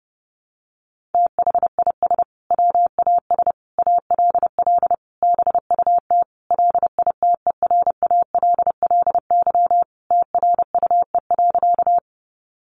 Morse code, heard as a keyed tone, about 30 wpm, 700 Hz. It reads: THIS WAS ALL BUT LITERALLY TRUE.
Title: Sentences from Top 1400 Words 30wpm